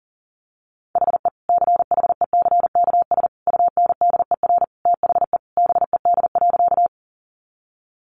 Morse code, heard as a keyed tone, about 40 wpm, 700 Hz. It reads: HE CHECKS UNDER THE BED.
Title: Easy Phrases 40wpm